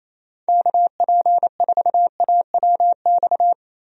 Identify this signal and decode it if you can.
Morse code, heard as a keyed tone, about 28 wpm, 700 Hz. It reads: KP4AWX
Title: DXCC-100 Call Signs - 6 Characters - Repeated 3x - 28wpm